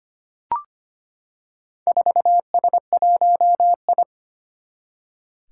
Morse code, heard as a keyed tone, about 25 wpm, 700 Hz. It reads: 4S1I
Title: Four Challenging Letter-Number Combinations 25wpm